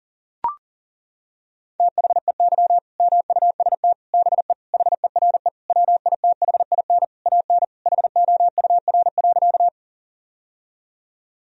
Morse code, heard as a keyed tone, about 40 wpm, 700 Hz. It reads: THEY MUST BE HERE WITHIN AN HOUR.
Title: Sentences from Top 1000 Words - Review Part 4 - 40wpm